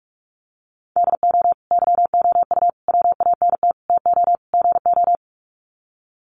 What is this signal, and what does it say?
Morse code, heard as a keyed tone, about 44 wpm, 700 Hz. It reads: DO YOU WANT TO GO
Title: Sets of 5 Words Encore2 45wpm